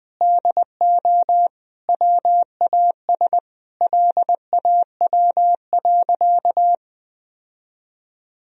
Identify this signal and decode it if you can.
Morse code, heard as a keyed tone, about 20 wpm, 700 Hz. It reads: DO WAS LAW.
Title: Sentences from Top 1000 Words - Review Part 3 - 20wpm